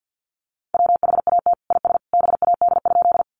Morse code, heard as a keyed tone, about 50 wpm, 700 Hz. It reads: WHAT IS BAD?
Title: Sentences from Top 700 Words 50wpm